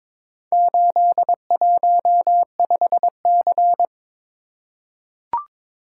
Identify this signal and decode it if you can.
Morse code, heard as a keyed tone, about 22 wpm, 700 Hz. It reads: 815C
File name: practice / 22wpm / Four Letter-Number Combinations - 22wpm